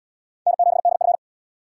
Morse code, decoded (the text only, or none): N7DF